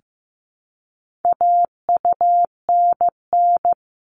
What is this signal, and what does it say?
Morse code, heard as a keyed tone, about 15 wpm, 700 Hz. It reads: AUNN
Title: All Cut Numbers - 4-Digit 15wpm